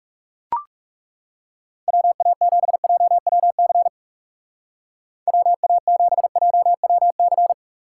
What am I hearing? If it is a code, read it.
Morse code, heard as a keyed tone, about 38 wpm, 700 Hz. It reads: WA7JWC WA7JWC